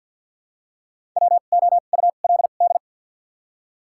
Morse code, heard as a keyed tone, about 53 wpm, spaced as for 37 wpm, 690 Hz. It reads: WOULD
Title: 2x Character Spacing - Top 100 Words - 50wpm